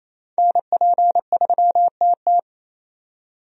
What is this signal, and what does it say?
Morse code, heard as a keyed tone, about 28 wpm, 700 Hz. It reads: NP3TT
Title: USA Call Signs - 5 Characters - 28wpm